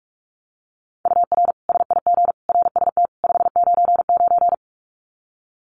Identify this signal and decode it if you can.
Morse code, heard as a keyed tone, about 44 wpm, 700 Hz. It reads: UR SIG RST 599
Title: Common QSO Phrases 45wpm